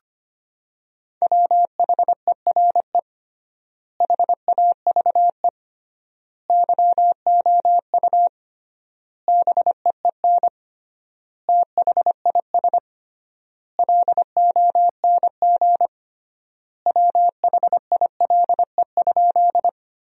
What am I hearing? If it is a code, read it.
Morse code, heard as a keyed tone, about 25 wpm, 700 Hz. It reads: WHERE HAVE YOU BEEN THIS LONG WHILE?
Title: 3x Word-Spacing - Sentences from Top 300 Words - 25wpm